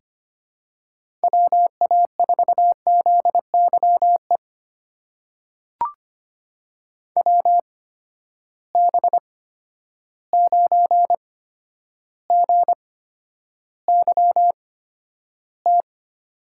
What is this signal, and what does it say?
Morse code, heard as a keyed tone, about 25 wpm, spaced as for 11 wpm, 700 Hz. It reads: WA4ZYE WB9GYT
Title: USA Call Signs - 6 Characters - 8x Character Spacing - 25wpm